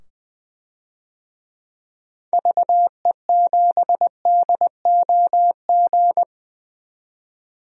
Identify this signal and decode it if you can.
Morse code, heard as a keyed tone, about 20 wpm, 700 Hz. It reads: VE7DOG
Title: DXCC-100 Call Signs - 6 Characters - 20wpm